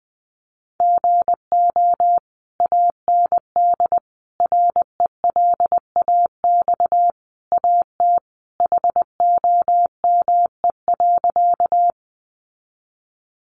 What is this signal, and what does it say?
Morse code, heard as a keyed tone, about 20 wpm, 700 Hz. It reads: GO AND RELAX AT HOME.
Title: Sentences from Top 1400 Words 20wpm